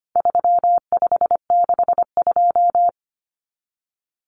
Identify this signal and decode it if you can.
Morse code, heard as a keyed tone, about 25 wpm, 700 Hz. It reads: 3562
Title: Numbers - 4-Digit 25wpm